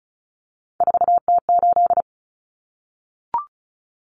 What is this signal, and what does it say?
Morse code, heard as a keyed tone, about 34 wpm, 700 Hz. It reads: 4T8
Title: Common Cut Numbers - 3-Digit 35wpm